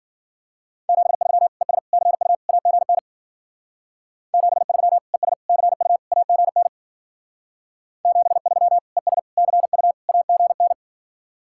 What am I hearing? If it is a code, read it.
Morse code, heard as a keyed tone, about 50 wpm, 700 Hz. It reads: 73 ES CU AGN 73 ES CU AGN 73 ES CU AGN
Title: Speed-Racing Common QSO Phrases Encore - 40wpm